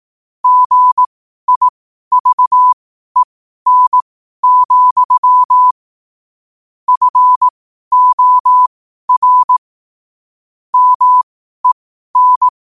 Morse code, heard as a keyed tone, about 18 wpm, spaced as for 12 wpm, 1000 Hz. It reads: GIVEN, FOR MEN
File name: qotdfiles / 12wpm